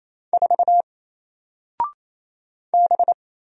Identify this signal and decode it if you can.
Morse code, heard as a keyed tone, about 27 wpm, 700 Hz. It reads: 4 B
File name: Single-Letter-Number-28wpm